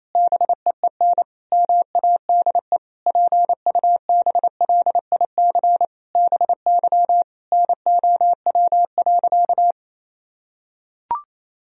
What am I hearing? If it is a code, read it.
Morse code, heard as a keyed tone, about 28 wpm, 700 Hz. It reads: BEEN MADE PUBLIC BY NOW.
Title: Sentences from Top 1000 Words - Review Part 3 - 28wpm